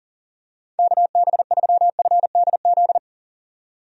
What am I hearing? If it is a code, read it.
Morse code, heard as a keyed tone, about 40 wpm, 700 Hz. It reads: KB3FDZ
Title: USA Call Signs - 6 Characters - 40wpm